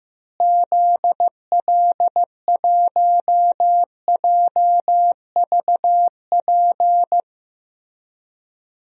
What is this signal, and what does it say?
Morse code, heard as a keyed tone, about 15 wpm, 700 Hz. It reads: ZL1JVP
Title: DXCC-100 Call Signs - 6 Characters - Repeated 3x - 15wpm